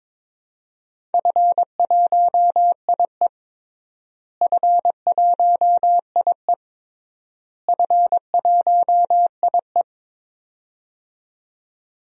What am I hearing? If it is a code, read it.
Morse code, heard as a keyed tone, about 22 wpm, 700 Hz. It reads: F1IE F1IE F1IE
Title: DXCC-100 Call Signs - 4 Characters - Repeated 3x - 22wpm